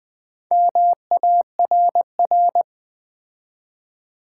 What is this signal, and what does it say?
Morse code, heard as a keyed tone, about 20 wpm, 700 Hz. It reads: MARR
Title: Build A Word - 4 Letters At A Time - 8-Letter Words - 20wpm